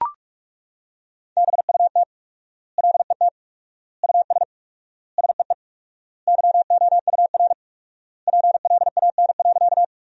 Morse code, heard as a keyed tone, about 41 wpm, 700 Hz. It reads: BUT LET US SEE YOUR PLAN.
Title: 4x Word-Spacing - Sentences from Top 500 Words - 45wpm